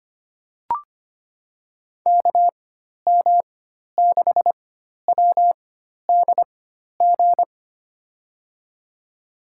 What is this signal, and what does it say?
Morse code, heard as a keyed tone, about 25 wpm, spaced as for 12 wpm, 700 Hz. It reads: KM6WDG
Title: USA Call Signs - 6 Characters - 4x Character Spacing - 25wpm